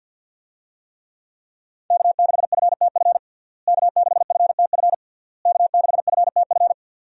Morse code, heard as a keyed tone, about 53 wpm, 700 Hz. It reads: K6FTF K6FTF K6FTF
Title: USA Call Signs - 5 Characters - Repeated 3x - 50wpm